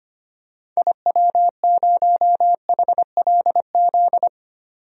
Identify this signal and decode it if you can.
Morse code, heard as a keyed tone, about 25 wpm, 700 Hz. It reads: IW0HLZ
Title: DXCC-100 Call Signs - 6 Characters - 25wpm